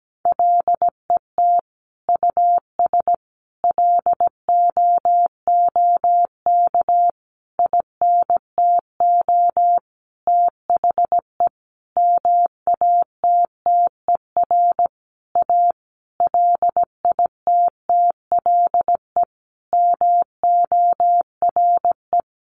Morse code, heard as a keyed tone, about 17 wpm, 700 Hz. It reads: LET US LOOK INTO THE MATTER A LITTLE MORE
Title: Sentences from Top 1400 Words 17wpm